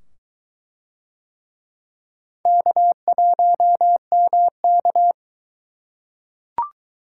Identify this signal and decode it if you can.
Morse code, heard as a keyed tone, about 21 wpm, 700 Hz. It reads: K1MK